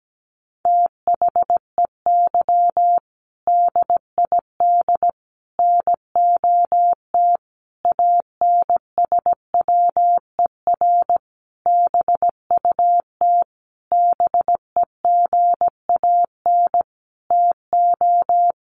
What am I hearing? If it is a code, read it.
Morse code, heard as a keyed tone, about 17 wpm, 700 Hz. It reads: THEY DID NOT ANSWER BUT BEGAN TO
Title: Sentences from Top 600 Words 17wpm